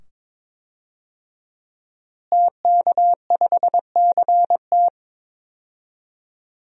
Morse code, heard as a keyed tone, about 22 wpm, 700 Hz. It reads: TK5CT
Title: DXCC-100 Call Signs - 5 Characters - Repeated 2x - 22wpm